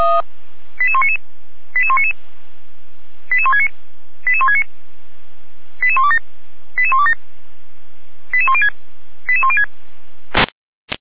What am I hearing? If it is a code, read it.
Five-tone selective call (ZVEI), heard as quick runs of five tones, tones of about 70 ms. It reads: 88199 88199 88169 88169 88127 88127 88106 88106
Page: feuerwehr-orlamuende